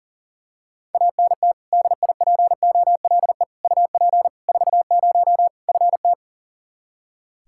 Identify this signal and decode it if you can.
Morse code, heard as a keyed tone, about 37 wpm, 700 Hz. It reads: ANT DIPOLE UP 40 FT
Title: Practice QSO - Exchange 2 - Info - 40wpm